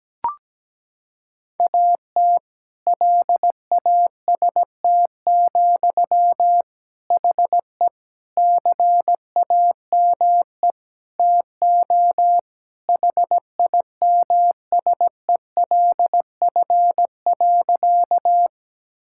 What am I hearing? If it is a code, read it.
Morse code, heard as a keyed tone, about 17 wpm, 700 Hz. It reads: AT LAST, HE CAME TO HIMSELF.
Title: Sentences from Top 1600 Words 17wpm